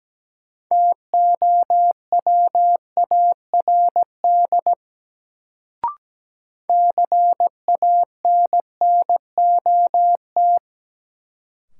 Morse code, heard as a keyed tone, about 17 wpm, 700 Hz. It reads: TOWARD CANNOT
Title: Sentences from Top 1000 Words - Review Part 6 - New Word Drill - 17wpm